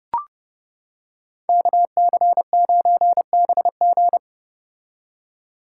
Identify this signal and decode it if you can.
Morse code, heard as a keyed tone, about 30 wpm, 700 Hz. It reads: KC9BG
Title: K1USN SST QSO Practice - 30wpm